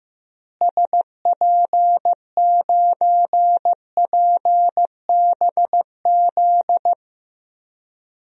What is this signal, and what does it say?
Morse code, heard as a keyed tone, about 15 wpm, 700 Hz. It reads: SP9PBZ
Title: DXCC-100 Call Signs - 6 Characters - Repeated 2x - 15wpm